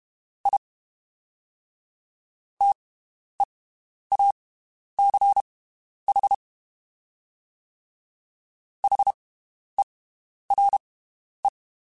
Morse code, heard as a keyed tone, about 32 wpm, spaced as for 11 wpm, 800 Hz. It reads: I TEACH HERE
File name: (spaced) / QSOs Spaced-mp3s / WD0BGP 1 @30 sp